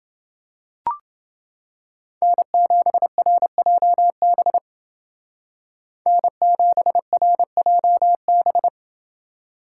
Morse code, heard as a keyed tone, about 29 wpm, 700 Hz. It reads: N7RJB N7RJB